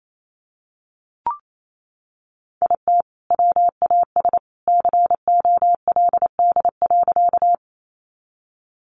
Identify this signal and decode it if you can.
Morse code, heard as a keyed tone, about 28 wpm, 700 Hz. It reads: IT WAS COLD.
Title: Sentences from Top 400 Words 28wpm